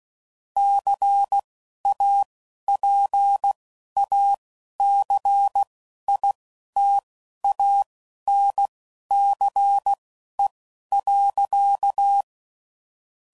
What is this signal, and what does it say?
Morse code, heard as a keyed tone, about 16 wpm, spaced as for 11 wpm, 800 Hz. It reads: CAPACITANCE.
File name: CW Code Groups @10WPM / Technical Subjects / XC